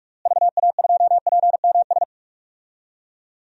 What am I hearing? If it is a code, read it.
Morse code, heard as a keyed tone, about 46 wpm, 700 Hz. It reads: VA2PMS
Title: DXCC-100 Call Signs - 6 Characters - Repeated 2x - 45wpm